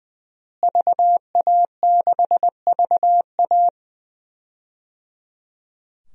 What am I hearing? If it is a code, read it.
Morse code, heard as a keyed tone, about 20 wpm, 700 Hz. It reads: VA6VA
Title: DXCC-100 Call Signs - 5 Characters - Repeated 3x - 20wpm